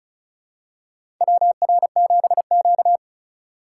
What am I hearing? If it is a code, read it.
Morse code, heard as a keyed tone, about 34 wpm, 700 Hz. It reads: WR7Q
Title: USA Call Signs - 4 Characters - 4x Character Spacing - 35wpm